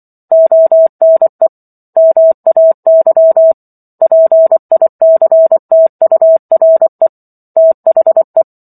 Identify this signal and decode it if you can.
Morse code, heard as a keyed tone, about 24 wpm, 650 Hz. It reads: ONE MAY PICTURE THE